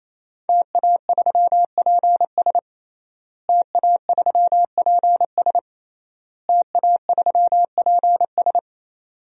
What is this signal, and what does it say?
Morse code, heard as a keyed tone, about 28 wpm, 700 Hz. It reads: TA3PS TA3PS TA3PS
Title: DXCC-100 Call Signs - 5 Characters - Repeated 3x - 28wpm